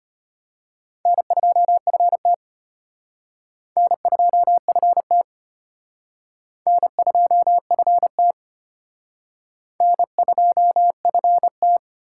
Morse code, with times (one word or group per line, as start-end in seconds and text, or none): 1.1-2.3 N2FT
3.8-5.2 N2FT
6.7-8.3 N2FT
9.8-11.8 N2FT